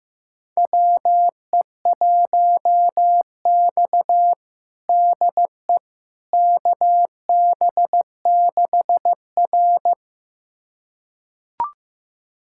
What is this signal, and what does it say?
Morse code, heard as a keyed tone, about 15 wpm, 700 Hz. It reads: WE1X DE KB6R
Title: USA Callsign DE Callsign - 4 Characters - 15wpm